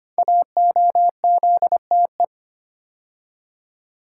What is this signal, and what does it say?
Morse code, heard as a keyed tone, about 25 wpm, 700 Hz. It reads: AOZTE
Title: Code Groups - 5 Letters - 25wpm